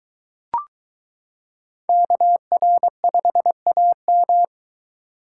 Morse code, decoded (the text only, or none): KR5AM